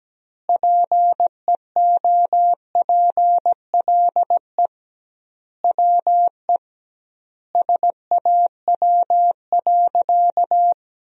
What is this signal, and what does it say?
Morse code, heard as a keyed tone, about 17 wpm, 700 Hz. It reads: PEOPLE WE SAW.
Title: 2x Word-Spacing - Sentences from Top 300 Words - 17wpm